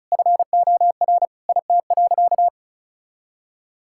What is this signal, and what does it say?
Morse code, heard as a keyed tone, about 34 wpm, 700 Hz. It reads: FOR IT.
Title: Sentences from Top 1600 Words 35wpm